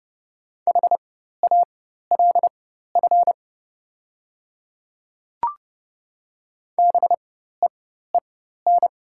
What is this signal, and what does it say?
Morse code, heard as a keyed tone, about 30 wpm, spaced as for 14 wpm, 690 Hz. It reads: HALF BEEN
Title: 4x Character Spacing - Top 500 Words - 30wpm